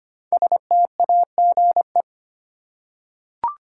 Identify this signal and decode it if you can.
Morse code, heard as a keyed tone, about 25 wpm, 700 Hz. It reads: STAGE
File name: 5 Letter Words - First 3 Letters Given - 25wpm